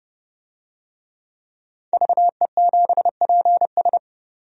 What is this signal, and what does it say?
Morse code, heard as a keyed tone, about 29 wpm, 700 Hz. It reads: VE7PS
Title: DXCC-100 Call Signs - 5 Characters - 30wpm